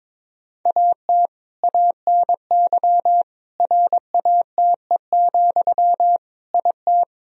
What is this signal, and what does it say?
Morse code, heard as a keyed tone, about 22 wpm, 700 Hz. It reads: AT ANY RATE, IT
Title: Sentences from Top 1000 Words - Review Part 4 - 22wpm